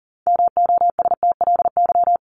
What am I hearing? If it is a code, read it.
Morse code, heard as a keyed tone, about 40 wpm, 700 Hz. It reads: MOSTLY
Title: LY Suffix - Top 100 Words with Base Words 40wpm